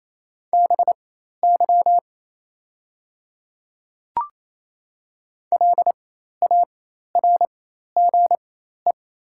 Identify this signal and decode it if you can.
Morse code, heard as a keyed tone, about 28 wpm, spaced as for 13 wpm, 700 Hz. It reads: BY LARGE
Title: 4x Character Spacing - Top 200 Words - 28wpm